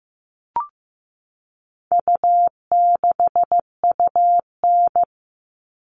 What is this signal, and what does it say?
Morse code, heard as a keyed tone, about 15 wpm, 700 Hz. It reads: U6UN
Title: All Cut Numbers - 4-Digit 15wpm